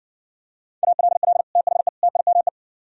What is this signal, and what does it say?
Morse code, heard as a keyed tone, about 60 wpm, 700 Hz. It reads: ALL THE TIME